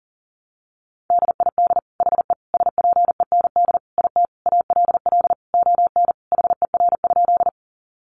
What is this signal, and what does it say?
Morse code, heard as a keyed tone, about 40 wpm, 700 Hz. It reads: DID HE SPEND IT ALL ON HER?
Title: Sentences from Top 800 Words 40wpm